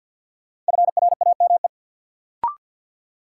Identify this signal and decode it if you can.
Morse code, heard as a keyed tone, about 52 wpm, 700 Hz. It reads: FRAME
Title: Sentences from Top 1000 Words - Review Part 6 - New Word Drill - 50wpm